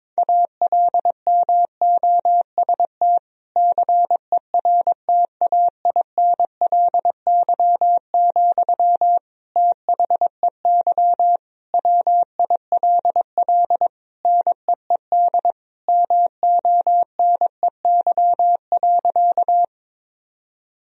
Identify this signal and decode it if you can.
Morse code, heard as a keyed tone, about 22 wpm, 700 Hz. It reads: ALMOST CERTAINLY, THEY WILL NEED MONEY.
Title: Sentences from Top 1000 Words - Review Part 4 - 22wpm